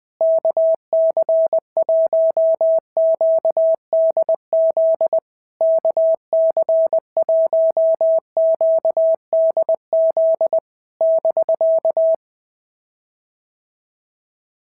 Morse code, heard as a keyed tone, about 20 wpm, 650 Hz. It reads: KC1QDZ KC1QDZ <BK>
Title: POTA QSO Practice - 20wpm